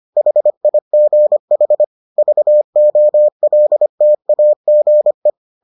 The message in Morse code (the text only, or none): HIGH VOLTAGE